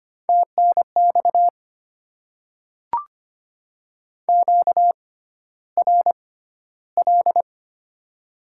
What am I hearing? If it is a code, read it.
Morse code, heard as a keyed tone, about 25 wpm, spaced as for 13 wpm, 700 Hz. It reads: TNX QRL
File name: Top 20 QSO Elements - 6x Character Spacing - 25wpm